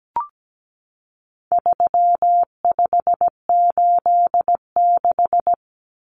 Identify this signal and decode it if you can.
Morse code, heard as a keyed tone, about 17 wpm, 700 Hz. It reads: 3586
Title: Numbers - 4-Digit 17wpm